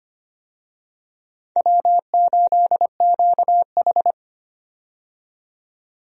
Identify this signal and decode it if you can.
Morse code, heard as a keyed tone, about 25 wpm, 700 Hz. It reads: W8QH
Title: USA Call Signs - 4 Characters - 25wpm